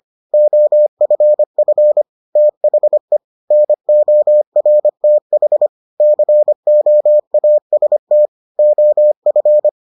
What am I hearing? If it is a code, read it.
Morse code, heard as a keyed tone, about 25 wpm, 600 Hz. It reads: OFF THE NORTH COAST OF